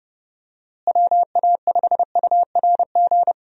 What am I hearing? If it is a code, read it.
Morse code, heard as a keyed tone, about 30 wpm, 700 Hz. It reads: WA5URG
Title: USA Call Signs - 6 Characters - Repeated 2x - 30wpm